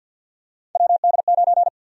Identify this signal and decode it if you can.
Morse code, heard as a keyed tone, about 51 wpm, 700 Hz. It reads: WD9
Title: USA Call Sign Prefixes - 50wpm